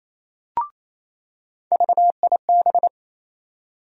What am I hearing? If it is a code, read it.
Morse code, heard as a keyed tone, about 28 wpm, 700 Hz. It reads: VIB
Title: USA Call Sign Postfixes - 3 Letters 28wpm